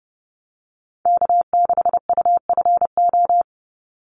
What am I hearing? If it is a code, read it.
Morse code, heard as a keyed tone, about 29 wpm, 700 Hz. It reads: K6UFO
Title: Call Signs Repeated 3x - 30wpm